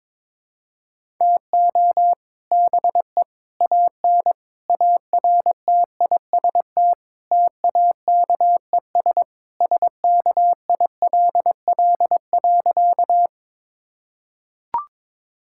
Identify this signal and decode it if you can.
Morse code, heard as a keyed tone, about 22 wpm, 700 Hz. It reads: TO BE AN ARTIST TAKES SKILL.
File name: practice / 22wpm / Sentences with 3-Word Phrases - 22wpm